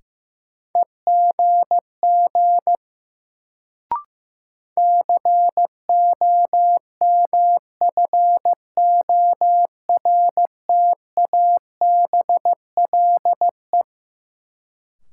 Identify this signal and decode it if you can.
Morse code, heard as a keyed tone, about 15 wpm, 700 Hz. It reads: EGG COMFORTABLE
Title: Sentences from Top 1000 Words - Review Part 7 - New Word Drill - 15wpm